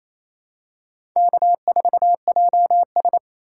Morse code, heard as a keyed tone, about 27 wpm, 700 Hz. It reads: K4JS
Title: USA Call Signs - 4 Characters - 6x Character Spacing - 28wpm